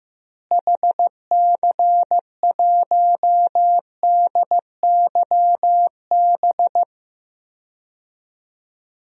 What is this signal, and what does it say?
Morse code, heard as a keyed tone, about 15 wpm, 700 Hz. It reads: HC1DYB
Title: DXCC-100 Call Signs - 6 Characters - Repeated 2x - 15wpm